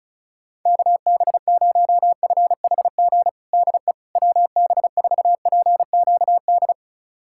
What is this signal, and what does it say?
Morse code, heard as a keyed tone, about 36 wpm, 700 Hz. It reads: KB0FHG DE WB4PQD
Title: USA Callsign DE Callsign - 6 Characters - 35wpm